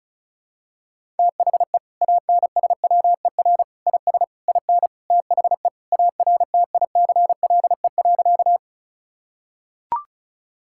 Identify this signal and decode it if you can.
Morse code, heard as a keyed tone, about 36 wpm, 700 Hz. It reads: THE ANSWER IS IN THE ARTICLE.
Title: Sentences from Top 1000 Words - Review Part 5 - 35wpm